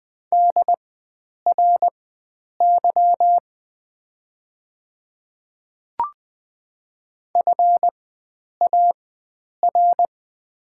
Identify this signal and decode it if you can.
Morse code, heard as a keyed tone, about 20 wpm, spaced as for 9 wpm, 700 Hz. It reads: DRY FAR
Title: 4x Character Spacing - Top 500 Words - 20wpm